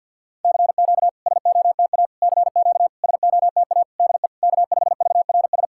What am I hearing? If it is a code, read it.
Morse code, heard as a keyed tone, about 50 wpm, 700 Hz. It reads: CQ SOTA CQ SOTA DE K5VRS
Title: SOTA QSO Practice 50wpm